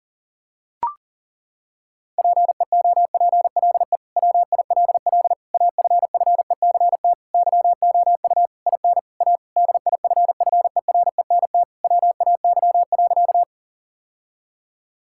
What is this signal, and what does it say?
Morse code, heard as a keyed tone, about 40 wpm, 700 Hz. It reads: PEOPLE WILL AFFECT YOU IN A DIFFERENT WAY.
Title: Sentences from Top 1000 Words - Review Part 6 - 40wpm